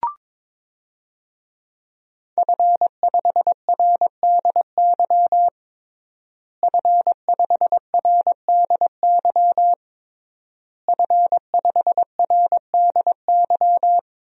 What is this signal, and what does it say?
Morse code, heard as a keyed tone, about 22 wpm, 700 Hz. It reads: F5RDY F5RDY F5RDY
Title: DXCC-100 Call Signs - 5 Characters - Repeated 3x - 22wpm